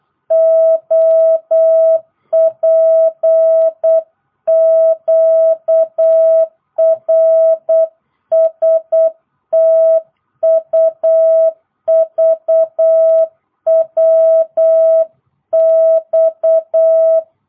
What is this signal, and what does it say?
Morse code, heard as a keyed tone, about 8 wpm, 650 Hz. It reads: OPQRSTUVWX